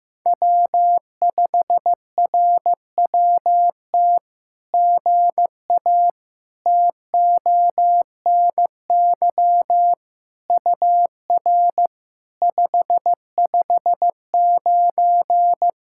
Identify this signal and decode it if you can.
Morse code, heard as a keyed tone, about 15 wpm, 700 Hz. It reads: W5RWT GA TONY UR 559
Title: SOTA QSO Practice 15wpm